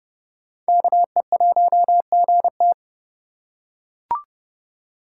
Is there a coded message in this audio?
Yes: Morse code from 0.7 to 2.7 s, heard as a keyed tone, about 29 wpm, 700 Hz. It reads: KE1GT